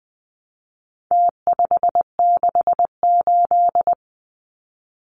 Morse code, heard as a keyed tone, about 20 wpm, 700 Hz. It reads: T568
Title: Numbers - Contest Style - 4-Digit 20wpm